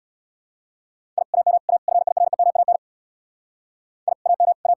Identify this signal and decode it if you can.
Morse code, heard as a keyed tone, about 72 wpm, 700 Hz. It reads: I AM A FIREMAN I AM A